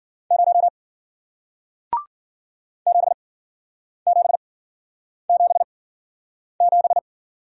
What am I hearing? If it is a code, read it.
Morse code, heard as a keyed tone, about 55 wpm, spaced as for 9 wpm, 700 Hz. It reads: 0 7777